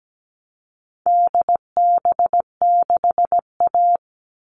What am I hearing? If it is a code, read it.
Morse code, heard as a keyed tone, about 17 wpm, 700 Hz. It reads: DB6A
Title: All Cut Numbers - 4-Digit 17wpm